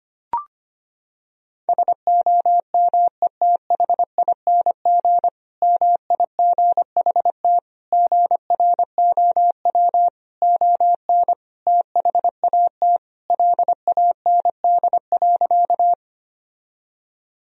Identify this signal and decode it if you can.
Morse code, heard as a keyed tone, about 25 wpm, 700 Hz. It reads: SOMETHING MIGHT GROW ON THAT LAND.
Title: Sentences from Top 1000 Words - Review Part 1 - 25wpm